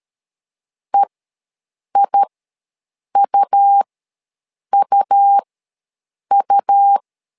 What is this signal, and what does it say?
Morse code, heard as a keyed tone, about 13 wpm, spaced as for 7 wpm, 800 Hz. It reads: EIUUU
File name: LEZIONI / 2 UHDN / 2 U ws 12wpm 1655